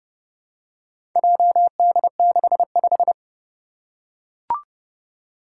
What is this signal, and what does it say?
Morse code, heard as a keyed tone, about 30 wpm, 700 Hz. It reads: JD65
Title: Four Challenging Letter-Number Combinations 30wpm